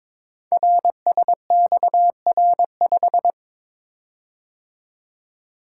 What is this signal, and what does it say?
Morse code, heard as a keyed tone, about 22 wpm, 700 Hz. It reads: RSXR5
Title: Code Groups - 5 Letter:Numbers - 22wpm